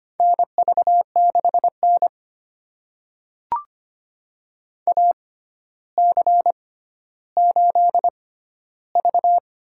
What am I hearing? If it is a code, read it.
Morse code, heard as a keyed tone, about 25 wpm, spaced as for 13 wpm, 700 Hz. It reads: NV6N AC8V
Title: USA Call Signs - 4 Characters - 6x Character Spacing - 25wpm